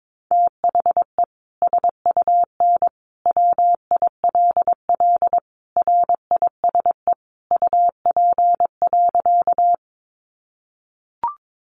Morse code, heard as a keyed tone, about 22 wpm, 700 Hz. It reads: THE SUN WILL RISE UP.
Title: Sentences from Top 900 Words 22wpm